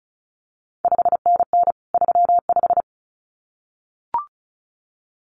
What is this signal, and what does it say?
Morse code, heard as a keyed tone, about 36 wpm, 700 Hz. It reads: 5NN 35
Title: CQ WW DX Contest Exchange Encore - 35wpm